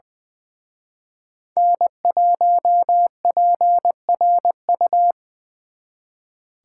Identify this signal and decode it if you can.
Morse code, heard as a keyed tone, about 23 wpm, 700 Hz. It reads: N1PRU
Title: Speed Racing - USA 1x3 Call Signs 20wpm